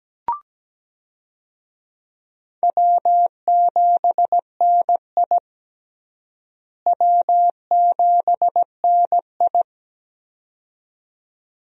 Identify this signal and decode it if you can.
Morse code, heard as a keyed tone, about 17 wpm, 700 Hz. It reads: W7NI W7NI
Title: USA Call Signs - 4 Characters - Repeated 2x - 17wpm